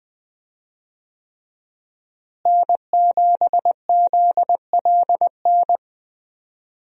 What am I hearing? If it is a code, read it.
Morse code, heard as a keyed tone, about 20 wpm, 700 Hz. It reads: N7ZLN
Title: USA Call Signs - 5 Characters - Repeated 3x - 20wpm